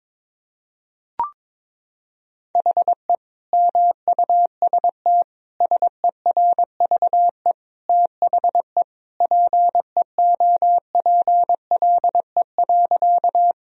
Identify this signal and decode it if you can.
Morse code, heard as a keyed tone, about 22 wpm, 700 Hz. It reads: HE MUST SERVE THE PEOPLE.
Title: Sentences from Top 1000 Words - Review Part 2 - 22wpm